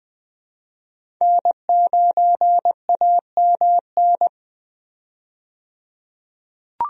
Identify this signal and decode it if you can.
Morse code, heard as a keyed tone, about 20 wpm, 700 Hz. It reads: N9AMN
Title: USA Call Signs - 5 Characters - 20wpm